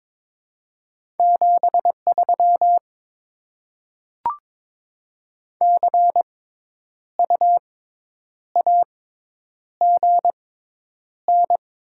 Morse code, heard as a keyed tone, about 21 wpm, spaced as for 9 wpm, 700 Hz. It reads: 73 CU AGN
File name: Top 20 QSO Elements - 6x Character Spacing - 22wpm